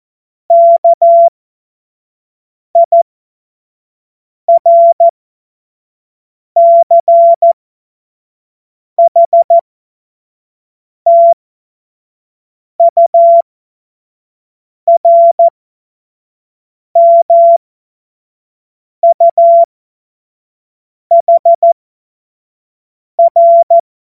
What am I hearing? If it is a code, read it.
Morse code, heard as a keyed tone, about 14 wpm, spaced as for 5 wpm, 680 Hz. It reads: KIRCHTURMUHR
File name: Morsekurs / Witze / Witz 19 Dezember 2023 25 F70